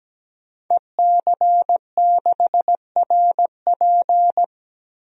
Morse code, heard as a keyed tone, about 17 wpm, 700 Hz. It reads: EC6RP
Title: DXCC-100 Call Signs - 5 Characters - Repeated 3x - 17wpm